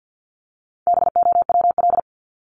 Morse code, heard as a keyed tone, about 50 wpm, 700 Hz. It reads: BOWL
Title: Top 1001-1250 Words 50wpm